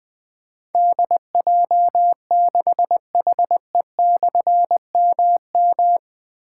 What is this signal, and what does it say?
Morse code, heard as a keyed tone, about 20 wpm, 700 Hz. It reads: DJ6HE/MM
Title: Difficult Call Signs 20wpm